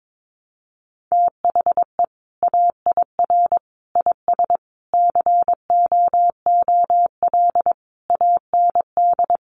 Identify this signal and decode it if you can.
Morse code, heard as a keyed tone, about 22 wpm, 700 Hz. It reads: THE AIR IS COOL AND
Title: Sentences from Top 1200 Words 22wpm